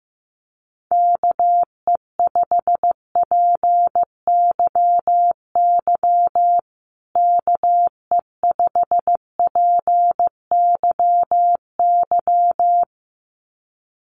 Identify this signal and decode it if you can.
Morse code, heard as a keyed tone, about 14 wpm, 700 Hz. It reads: KE5PYY KE5PYY
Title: USA Call Signs Repeated 2x 15wpm